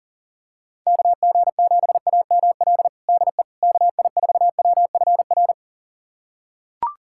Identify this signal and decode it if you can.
Morse code, heard as a keyed tone, about 40 wpm, 700 Hz. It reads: KG7AML DE KI4WFR
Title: USA Callsign DE Callsign 40wpm